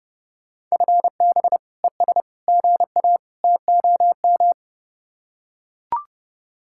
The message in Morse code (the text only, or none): FB ES GA TOM